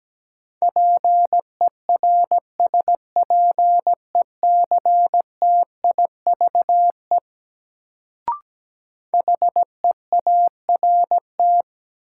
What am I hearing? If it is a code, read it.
Morse code, heard as a keyed tone, about 17 wpm, 700 Hz. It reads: PERSPECTIVE HEART
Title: Sentences from Top 1200 Words - New Word Drill - 17wpm